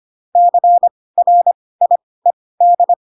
Morse code, heard as a keyed tone, about 25 wpm, spaced as for 18 wpm, 700 Hz. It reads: CRIED